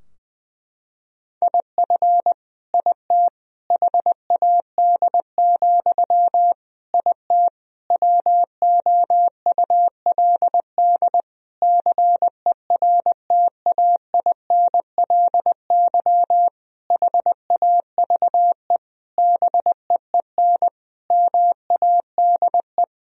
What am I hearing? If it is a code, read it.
Morse code, heard as a keyed tone, about 20 wpm, 700 Hz. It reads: IF IT HAD, IT WOULD CERTAINLY HAVE BEEN MADE
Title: Sentences from Top 1000 Words - Review Part 3 - 20wpm